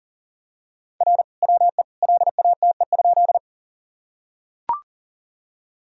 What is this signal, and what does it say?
Morse code, heard as a keyed tone, about 40 wpm, 700 Hz. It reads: R WE LATE?
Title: Sentences from Top 300 Words and Abbreviations 40wpm